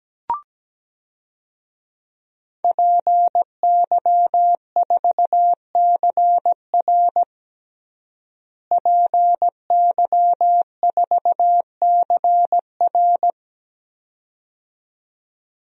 Morse code, heard as a keyed tone, about 17 wpm, 700 Hz. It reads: PY4CR PY4CR
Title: DXCC-100 Call Signs - 5 Characters - Repeated 2x - 17wpm